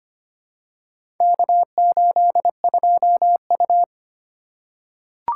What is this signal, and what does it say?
Morse code, heard as a keyed tone, about 25 wpm, 700 Hz. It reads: K82U
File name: practice / 25wpm / Four Letter-Number Combinations - 25wpm